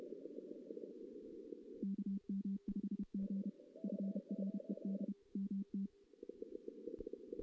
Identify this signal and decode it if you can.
Morse code, heard as a keyed tone, about 31 wpm, 200 Hz. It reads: KM5G FRED MT
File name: 2016 JAN CW NAQP